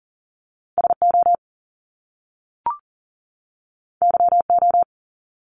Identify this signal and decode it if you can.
Morse code, heard as a keyed tone, about 38 wpm, 700 Hz. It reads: SO YO
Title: Top Two Letter Combinations in Words 40wpm